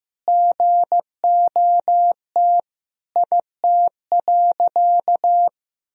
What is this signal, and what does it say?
Morse code, heard as a keyed tone, about 15 wpm, 700 Hz. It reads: GOT IT.
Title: Sentences from Top 300 Words and Abbreviations 15wpm